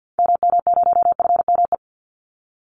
Morse code, heard as a keyed tone, about 50 wpm, 700 Hz. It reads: MM0FME
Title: Call Signs - 50wpm